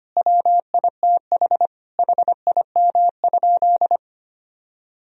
Morse code, heard as a keyed tone, about 25 wpm, 700 Hz. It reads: WITH HIM?
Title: Sentences from Top 1000 Words - Review Part 1 - 25wpm